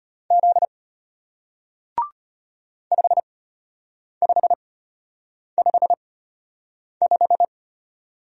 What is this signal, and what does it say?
Morse code, heard as a keyed tone, about 33 wpm, spaced as for 8 wpm, 690 Hz. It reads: Z 5555